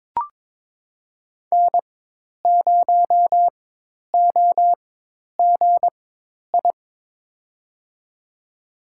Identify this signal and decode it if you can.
Morse code, heard as a keyed tone, about 22 wpm, spaced as for 10 wpm, 700 Hz. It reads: N0OGI
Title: USA Call Signs - 5 Characters - 4x Character Spacing - 22wpm